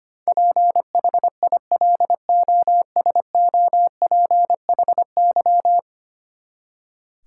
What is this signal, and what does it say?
Morse code, heard as a keyed tone, about 25 wpm, 700 Hz. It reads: PHILOSOPHY
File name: Ten Letter Words - 25wpm